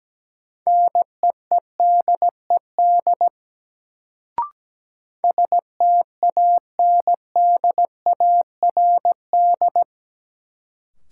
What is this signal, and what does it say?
Morse code, heard as a keyed tone, about 17 wpm, 700 Hz. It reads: NEEDED STANDARD
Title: Sentences from Top 1000 Words - Review Part 6 - New Word Drill - 17wpm